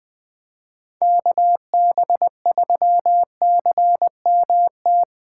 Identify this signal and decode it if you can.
Morse code, heard as a keyed tone, about 20 wpm, 700 Hz. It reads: KB3CMT
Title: 2023 POTA Activator Call Signs - 20wpm